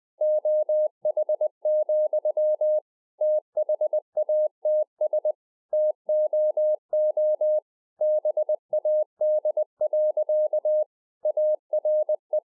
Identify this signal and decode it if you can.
Morse code, heard as a keyed tone, about 20 wpm, 610 Hz. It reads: OH, THATS TOO BAD. ARE